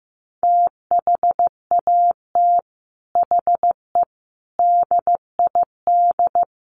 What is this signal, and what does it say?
Morse code, heard as a keyed tone, about 15 wpm, 700 Hz. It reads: THAT HE DID
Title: Sets of 4 Words 15wpm